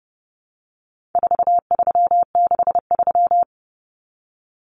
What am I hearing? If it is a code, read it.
Morse code, heard as a keyed tone, about 30 wpm, 700 Hz. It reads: 4363
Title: Common Cut Numbers - 4-Digit 30wpm